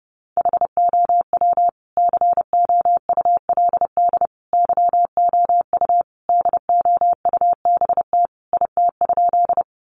Morse code, heard as a keyed tone, about 30 wpm, 700 Hz. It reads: HOW COULD YOU DOUBT IT?
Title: Sentences from Top 1500 Words 30wpm